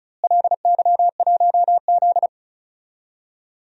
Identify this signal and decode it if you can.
Morse code, heard as a keyed tone, about 34 wpm, 700 Hz. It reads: LY1Z
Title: DXCC-100 Call Signs - 4 Characters - Repeated 3x - 35wpm